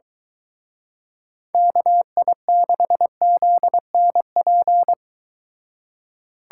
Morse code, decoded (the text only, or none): KI6ZNP